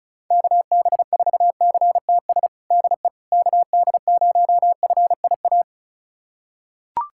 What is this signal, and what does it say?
Morse code, heard as a keyed tone, about 36 wpm, 700 Hz. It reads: KB4CTS DE KD0FIA
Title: USA Callsign DE Callsign 35wpm